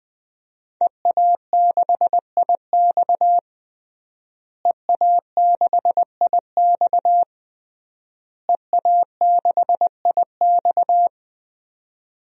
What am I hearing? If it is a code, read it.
Morse code, heard as a keyed tone, about 20 wpm, 700 Hz. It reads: EA6IX EA6IX EA6IX
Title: DXCC-100 Call Signs - 5 Characters - Repeated 3x - 20wpm